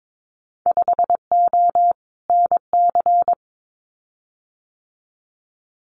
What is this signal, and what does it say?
Morse code, heard as a keyed tone, about 22 wpm, 700 Hz. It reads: 5O NC
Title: Winter Field Day Exchanges 22wpm